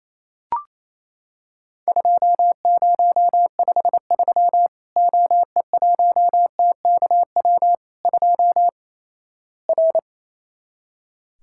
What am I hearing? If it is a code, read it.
Morse code, heard as a keyed tone, about 28 wpm, 700 Hz. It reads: 2053 OE1TKW 2 R
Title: Worked All Europe DX Contest - QTC Exchange - 28wpm